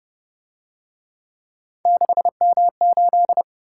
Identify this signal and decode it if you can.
Morse code, heard as a keyed tone, about 28 wpm, 700 Hz. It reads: 6M8